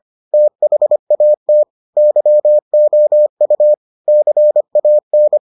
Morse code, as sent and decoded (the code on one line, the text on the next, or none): - .... .- - / -.-- --- ..- / -.-. .- -.
THAT YOU CAN